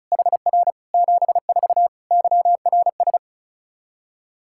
Morse code, heard as a keyed tone, about 36 wpm, 700 Hz. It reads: HR 74 YRS
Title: Practice QSO - Exchange 3 - More Info - 35wpm